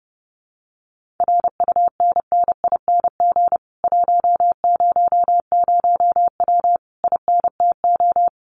Morse code, heard as a keyed tone, about 30 wpm, 700 Hz. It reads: RUNNING 100W INTO
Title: Common QSO Phrases Encore 30wpm